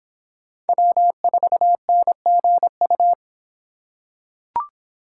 Morse code, heard as a keyed tone, about 25 wpm, 700 Hz. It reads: W4NGU